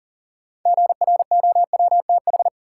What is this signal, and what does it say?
Morse code, heard as a keyed tone, about 40 wpm, 700 Hz. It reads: GROWTH
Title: Sentences from Top 1300 Words - New Word Drill Alt - 40wpm